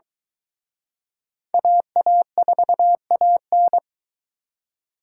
Morse code, heard as a keyed tone, about 22 wpm, 700 Hz. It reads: AA4AN